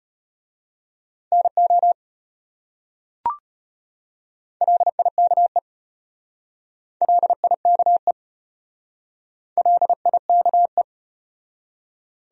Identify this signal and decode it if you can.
Morse code, heard as a keyed tone, about 34 wpm, 700 Hz. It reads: NO LIKE LIKE LIKE